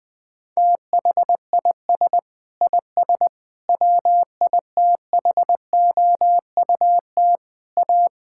Morse code, heard as a keyed tone, about 20 wpm, 700 Hz. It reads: THIS IS WITHOUT A
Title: Sentences with 7-Word Phrases - Spoken Beginning - 20wpm